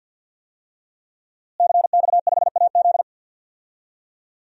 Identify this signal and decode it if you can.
Morse code, heard as a keyed tone, about 52 wpm, 700 Hz. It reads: CX5AZ
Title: DXCC-100 Call Signs - 5 Characters - Repeated 3x - 50wpm